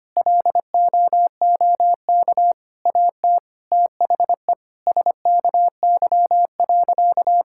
Morse code, heard as a keyed tone, about 25 wpm, 700 Hz. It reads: LOOK AT THE SKY.
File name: practice / 25wpm / Sentences with 9-Word Phrases - 25wpm